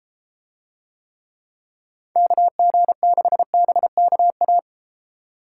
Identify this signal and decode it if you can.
Morse code, heard as a keyed tone, about 32 wpm, 700 Hz. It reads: KG6BKA